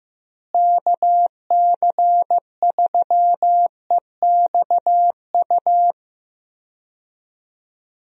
Morse code, heard as a keyed tone, about 15 wpm, 700 Hz. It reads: KC3EXU
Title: USA Call Signs - 6 Characters - Repeated 2x - 15wpm